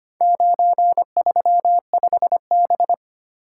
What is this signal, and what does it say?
Morse code, heard as a keyed tone, about 25 wpm, 700 Hz. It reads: 935B
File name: practice / 25wpm / Four Letter-Number Combinations - 25wpm